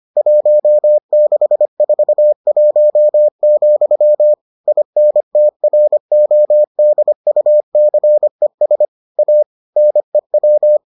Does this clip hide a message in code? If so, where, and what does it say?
Yes: Morse code from 0.2 to 10.8 s, heard as a keyed tone, about 25 wpm, 600 Hz. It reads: 1641, INTRODUCES A NEW